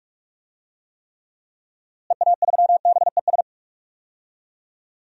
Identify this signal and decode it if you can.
Morse code, heard as a keyed tone, about 46 wpm, 700 Hz. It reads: EA3BES
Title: DXCC-100 Call Signs - 6 Characters - Repeated 3x - 45wpm